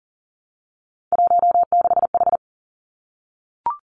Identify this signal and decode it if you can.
Morse code, heard as a keyed tone, about 40 wpm, 700 Hz. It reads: 16H
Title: Three Challenging Letter-Number Combinations 40wpm